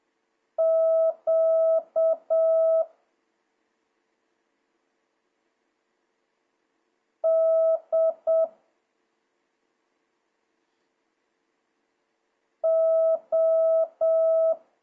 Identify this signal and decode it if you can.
Morse code, heard as a keyed tone, about 7 wpm, 650 Hz. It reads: Q D O
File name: jh8cbh / 2026cw